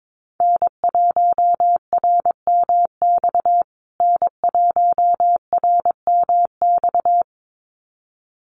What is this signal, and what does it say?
Morse code, heard as a keyed tone, about 22 wpm, 700 Hz. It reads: N1RMX N1RMX
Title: USA Call Signs Repeated 2x 22wpm